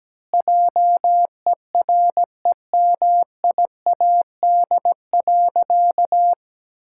Morse code, heard as a keyed tone, about 17 wpm, 700 Hz. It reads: JEREMIAD.
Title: Pangram Challenge v2 - 17wpm